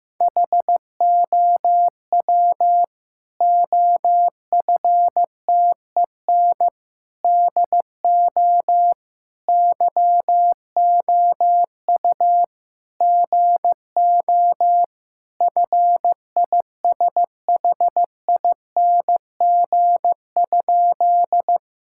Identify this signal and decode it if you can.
Morse code, heard as a keyed tone, about 15 wpm, 700 Hz. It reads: HOW OFTEN DO YOU GO FISHING?
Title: Sentences from Top 1000 Words - Review Part 5 - 15wpm